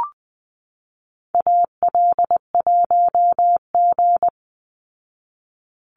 Morse code, heard as a keyed tone, about 20 wpm, 700 Hz. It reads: AL1G
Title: Call Signs Repeated 2x - 20wpm